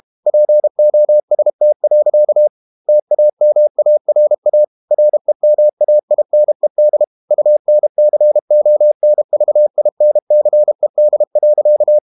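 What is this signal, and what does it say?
Morse code, heard as a keyed tone, about 32 wpm, 600 Hz. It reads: POST. TAMARA REMAINED UNCONVINCED.
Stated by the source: Morse code practice at 32 wpm from CW med Gnister, reading Ken Follett's novel Never in English.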